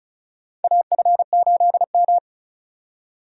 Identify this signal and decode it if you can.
Morse code, heard as a keyed tone, about 36 wpm, 700 Hz. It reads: AF8M
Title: USA Call Signs - 4 Characters - 35wpm